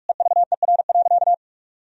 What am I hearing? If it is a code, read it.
Morse code, heard as a keyed tone, about 41 wpm, 700 Hz. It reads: EVER.
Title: Sentences from Top 1000 Words - Review Part 2 - 45wpm